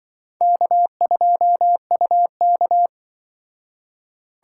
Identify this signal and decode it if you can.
Morse code, heard as a keyed tone, about 25 wpm, 700 Hz. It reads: K2UK
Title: Speed Racing - USA 1x2 Call Signs 20wpm